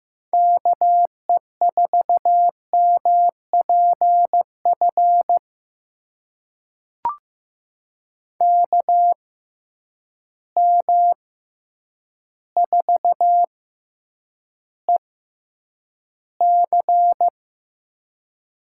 Morse code, heard as a keyed tone, about 15 wpm, spaced as for 8 wpm, 700 Hz. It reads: KE4MPF KM4EC
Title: USA Call Signs - 6x Character Spacing - 15wpm